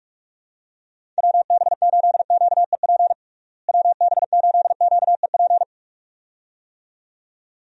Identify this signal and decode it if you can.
Morse code, heard as a keyed tone, about 46 wpm, 700 Hz. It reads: WB8QEP WB8QEP
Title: USA Call Signs - 6 Characters - Repeated 2x - 45wpm